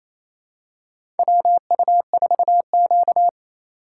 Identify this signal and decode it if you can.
Morse code, heard as a keyed tone, about 28 wpm, 700 Hz. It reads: WU4Q
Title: USA Call Signs - 4 Characters - 3x Character Spacing - 28wpm